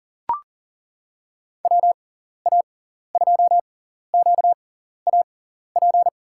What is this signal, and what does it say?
Morse code, heard as a keyed tone, about 40 wpm, spaced as for 14 wpm, 700 Hz. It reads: WA2QAP
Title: USA Call Signs - 6 Characters - 6x Character Spacing - 40wpm